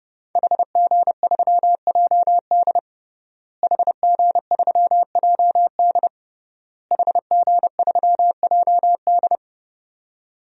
Morse code, heard as a keyed tone, about 32 wpm, 700 Hz. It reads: HG3JD HG3JD HG3JD
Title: DXCC-100 Call Signs - 5 Characters - Repeated 3x - 30wpm